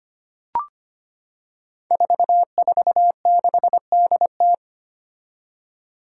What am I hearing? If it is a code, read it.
Morse code, heard as a keyed tone, about 25 wpm, 700 Hz. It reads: 446DT
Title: Five Challenging Letter-Number Combinations 25wpm